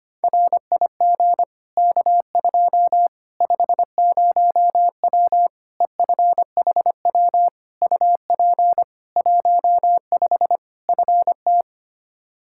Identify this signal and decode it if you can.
Morse code, heard as a keyed tone, about 25 wpm, 700 Hz. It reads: RIG K2 50W EFHW UP 15 FT
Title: Practice QSO - Exchange 2 - Info - 25wpm